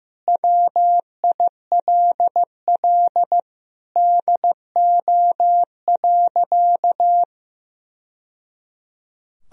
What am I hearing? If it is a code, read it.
Morse code, heard as a keyed tone, about 15 wpm, 700 Hz. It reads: WILL DO.
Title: Sentences from Top 1000 Words - Review Part 2 - 15wpm